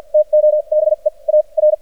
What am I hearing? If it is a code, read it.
Morse code, heard as a keyed tone, about 53 wpm, spaced as for 39 wpm, 610 Hz. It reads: TO BE AR
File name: archived / CRSnet